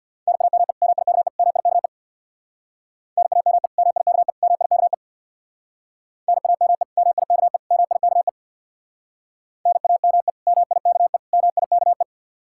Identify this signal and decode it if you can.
Morse code, heard as a keyed tone, about 60 wpm, 700 Hz. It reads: NAME MIKE MIKE NAME MIKE MIKE NAME MIKE MIKE NAME MIKE MIKE
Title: Speed-Racing Common QSO Phrases - 50wpm